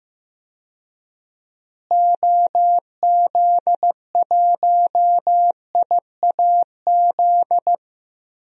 Morse code, heard as a keyed tone, about 15 wpm, 700 Hz. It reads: OZ1IAZ
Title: DXCC-100 Call Signs - 6 Characters - Repeated 2x - 15wpm